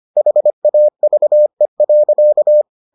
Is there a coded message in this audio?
Yes: Morse code from 0.2 to 2.6 s, heard as a keyed tone, about 25 wpm, 600 Hz. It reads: HAVE.